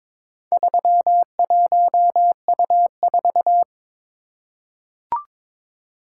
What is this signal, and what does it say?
Morse code, heard as a keyed tone, about 22 wpm, 690 Hz. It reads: 31U4
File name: practice / 22wpm / Four Letter-Number Combinations - 22wpm